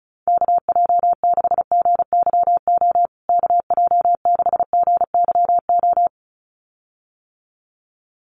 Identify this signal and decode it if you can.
Morse code, heard as a keyed tone, about 34 wpm, 700 Hz. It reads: KJ6GYO KJ6GYO
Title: USA Call Signs Repeated 2x 35wpm